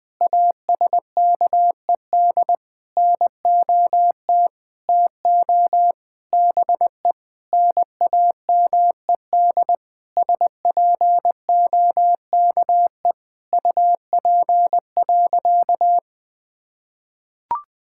Morse code, heard as a keyed tone, about 20 wpm, 700 Hz. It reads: ASKED NOT TO BE NAMED SPOKE UP.
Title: Sentences with 6-Word Phrases - 20wpm